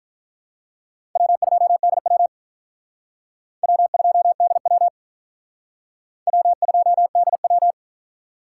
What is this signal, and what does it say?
Morse code, heard as a keyed tone, about 44 wpm, 700 Hz. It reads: W2DW W2DW W2DW